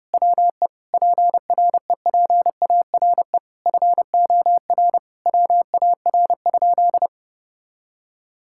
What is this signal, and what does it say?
Morse code, heard as a keyed tone, about 32 wpm, 700 Hz. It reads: WE PREPARE FOR WAR?
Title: Sentences from Top 1600 Words 30wpm